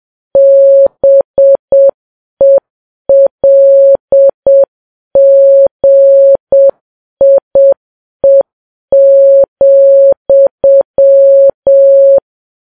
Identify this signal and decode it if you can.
Morse code, heard as a keyed tone, about 7 wpm, 550 Hz. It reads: BELGIE,
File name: Files Web tls / Morse on9fv / morse7kps